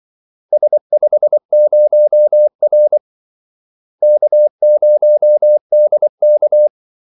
Morse code, heard as a keyed tone, about 24 wpm, 600 Hz. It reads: S50R K0DK
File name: Callsigns-24wpm